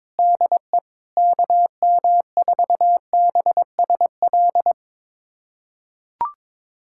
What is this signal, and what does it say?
Morse code, heard as a keyed tone, about 22 wpm, 700 Hz. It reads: DE KM4BSL
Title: USA Callsign DE Callsign - 6 Characters - 22wpm